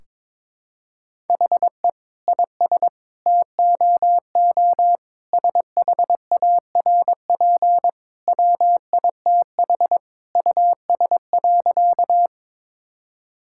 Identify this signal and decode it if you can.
Morse code, heard as a keyed tone, about 22 wpm, 700 Hz. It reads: HE IS TOO SHARP WITH US.
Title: Sentences from Top 1600 Words 22wpm